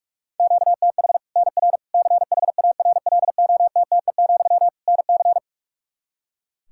Morse code, heard as a keyed tone, about 41 wpm, 700 Hz. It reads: QTH NR CHARLOTTE, NC
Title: Practice QSO - Exchange 1 - Beginning - 45wpm